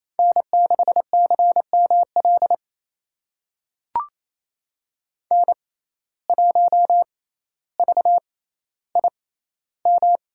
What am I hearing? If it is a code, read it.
Morse code, heard as a keyed tone, about 28 wpm, spaced as for 14 wpm, 700 Hz. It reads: N6CML N1VIM
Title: USA Call Signs - 5 Characters - 6x Character Spacing - 28wpm